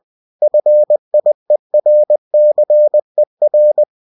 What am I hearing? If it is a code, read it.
Morse code, heard as a keyed tone, about 20 wpm, 600 Hz. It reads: FIERCER
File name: cw-books / winnie-the-pooh / winnie-20wpm-page-0062